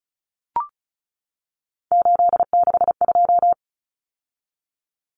Morse code, heard as a keyed tone, about 34 wpm, 700 Hz. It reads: 862
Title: Common Cut Numbers - 3-Digit 35wpm